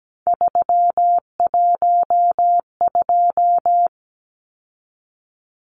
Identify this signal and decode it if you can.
Morse code, heard as a keyed tone, about 17 wpm, 700 Hz. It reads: 312
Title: Numbers - 3-Digit 17wpm